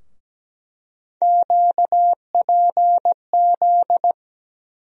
Morse code, heard as a keyed tone, about 17 wpm, 700 Hz. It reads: QPZ
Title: USA Call Sign Postfixes - 3 Letters 17wpm